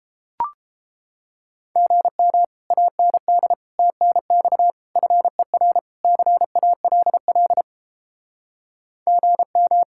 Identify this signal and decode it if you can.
Morse code, heard as a keyed tone, about 32 wpm, 700 Hz. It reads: GM AND TNX FER CALL GM